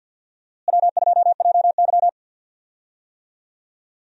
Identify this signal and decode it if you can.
Morse code, heard as a keyed tone, about 50 wpm, 700 Hz. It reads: W2JY
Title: USA Call Signs - 4 Characters - Repeated 3x - 50wpm